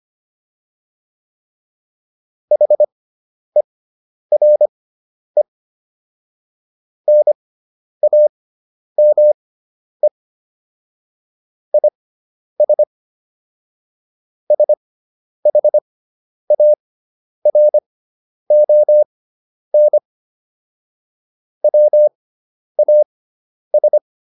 Morse code, heard as a keyed tone, about 25 wpm, spaced as for 10 wpm, 600 Hz. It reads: HERE NAME IS SHARON WAS